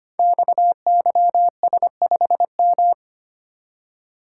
Morse code, heard as a keyed tone, about 25 wpm, 700 Hz. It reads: XYS5M
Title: Code Groups - 5 Letter:Numbers - 25wpm